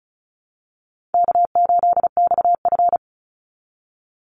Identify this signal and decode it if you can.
Morse code, heard as a keyed tone, about 34 wpm, 700 Hz. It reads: K8XF
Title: Easy Call Signs Repeated 2x - 35wpm